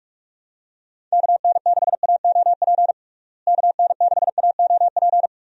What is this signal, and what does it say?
Morse code, heard as a keyed tone, about 46 wpm, 700 Hz. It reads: KN6AOP KN6AOP
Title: USA Call Signs - 6 Characters - Repeated 2x - 45wpm